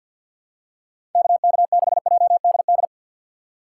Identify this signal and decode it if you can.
Morse code, heard as a keyed tone, about 53 wpm, 700 Hz. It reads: KK6JDD
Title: USA Call Signs - 6 Characters - 4x Character Spacing - 50wpm